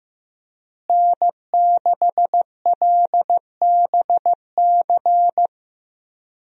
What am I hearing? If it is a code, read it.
Morse code, heard as a keyed tone, about 15 wpm, 700 Hz. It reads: N6LBC
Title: USA Call Signs - 5 Characters - Repeated 3x - 15wpm